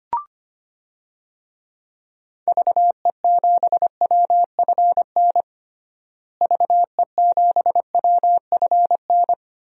Morse code, heard as a keyed tone, about 25 wpm, 700 Hz. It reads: VE7WFN VE7WFN
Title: DXCC-100 Call Signs - 6 Characters - Repeated 2x - 25wpm